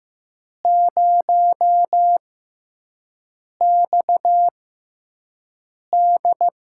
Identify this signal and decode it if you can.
Morse code, heard as a keyed tone, about 15 wpm, spaced as for 5 wpm, 700 Hz. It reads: 0XD
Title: USA Call Signs - 5 Characters - 6x Character Spacing - 15wpm